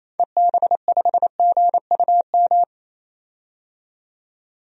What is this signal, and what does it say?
Morse code, heard as a keyed tone, about 28 wpm, 700 Hz. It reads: EB5GUM
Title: DXCC-100 Call Signs - 6 Characters - 28wpm